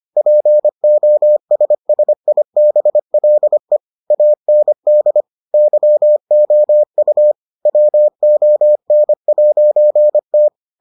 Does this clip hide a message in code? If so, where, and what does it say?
Yes: Morse code from 0.2 to 10.5 s, heard as a keyed tone, about 25 wpm, 600 Hz. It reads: POSSIBLE AND YOU WON'T